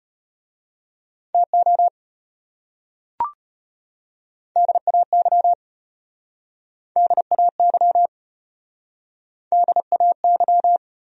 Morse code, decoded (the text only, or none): TO DAY DAY DAY